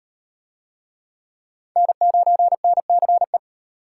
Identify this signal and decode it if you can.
Morse code, heard as a keyed tone, about 32 wpm, 700 Hz. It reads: N9NCE